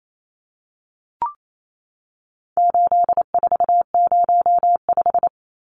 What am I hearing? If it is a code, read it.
Morse code, heard as a keyed tone, about 28 wpm, 700 Hz. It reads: 8405
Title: Numbers - 4-Digit 28wpm